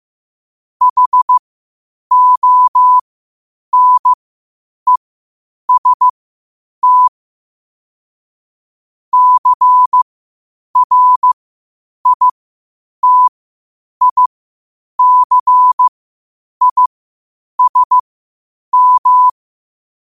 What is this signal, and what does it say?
Morse code, heard as a keyed tone, about 15 wpm, spaced as for 8 wpm, 1000 Hz. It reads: HONEST CRITICISM
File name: qotdfiles / 8wpm